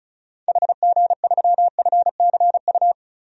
Morse code, heard as a keyed tone, about 36 wpm, 700 Hz. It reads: HG3FCU
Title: DXCC-100 Call Signs - 6 Characters - Repeated 3x - 35wpm